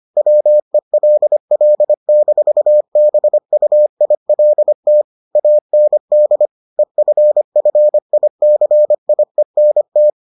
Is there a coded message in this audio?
Yes: Morse code from 0.2 to 10.1 s, heard as a keyed tone, about 25 wpm, 600 Hz. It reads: WELL-BUILT AND EFFICIENT